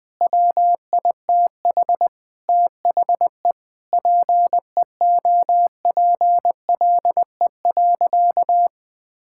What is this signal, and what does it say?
Morse code, heard as a keyed tone, about 20 wpm, 700 Hz. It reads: WITH THE PEOPLE.
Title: Sentences from Top 1000 Words - Review Part 5 - 20wpm